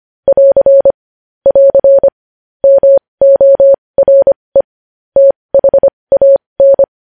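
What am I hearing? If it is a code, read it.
Morse code, heard as a keyed tone, about 25 wpm, spaced as for 20 wpm, 550 Hz. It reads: <AR> <AR> MORE THAN
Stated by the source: News headlines in Morse at 20 wpm.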